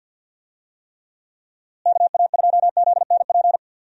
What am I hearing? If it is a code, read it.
Morse code, heard as a keyed tone, about 49 wpm, 700 Hz. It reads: KA2ZNP
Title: USA Call Signs - 6 Characters - Repeated 2x - 50wpm